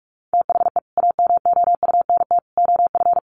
Morse code, heard as a keyed tone, about 44 wpm, 700 Hz. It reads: THE AMOUNT OF
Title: Sets of 3 Words 45wpm